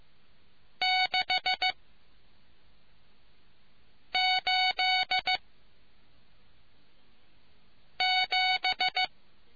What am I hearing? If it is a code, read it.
Morse code, heard as a keyed tone, about 15 wpm, 740 Hz. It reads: 6 8 7